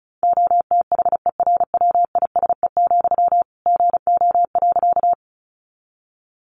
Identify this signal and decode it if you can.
Morse code, heard as a keyed tone, about 34 wpm, 700 Hz. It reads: OTHERWISE, GO.
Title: Sentences from Top 1200 Words 35wpm